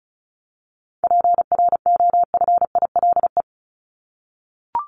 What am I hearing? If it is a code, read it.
Morse code, heard as a keyed tone, about 34 wpm, 700 Hz. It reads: PROFILE
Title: Top 1501-1750 Words 35wpm